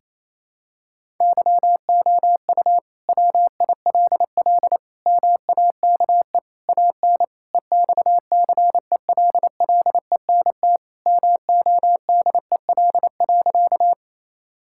Morse code, heard as a keyed tone, about 28 wpm, 700 Hz. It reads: YOU WILL MAKE AN EXCELLENT MODEL.
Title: Sentences from Top 1000 Words - Review Part 5 - 28wpm